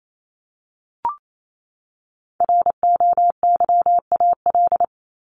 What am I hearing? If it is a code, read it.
Morse code, heard as a keyed tone, about 28 wpm, 700 Hz. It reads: ROYAL
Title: Top 1751-2000 Words 28wpm